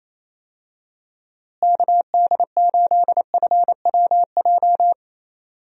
Morse code, heard as a keyed tone, about 28 wpm, 700 Hz. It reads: KD8FWJ
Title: USA Call Signs - 6 Characters - Repeated 3x - 28wpm